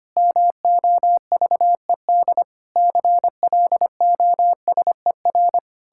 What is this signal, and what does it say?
Morse code, heard as a keyed tone, about 25 wpm, 700 Hz. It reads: MOVED CLOSER
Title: Pangram Challenge v2 - 25wpm